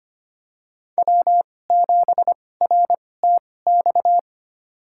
Morse code, heard as a keyed tone, about 25 wpm, spaced as for 18 wpm, 700 Hz. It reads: W7RTX
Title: USA Call Signs - 5 Characters - 2x Character Spacing - 25wpm